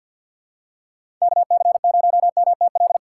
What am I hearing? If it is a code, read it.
Morse code, heard as a keyed tone, about 48 wpm, 700 Hz. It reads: KC0MTL
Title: USA Call Signs - 3x Character Spacing - 50wpm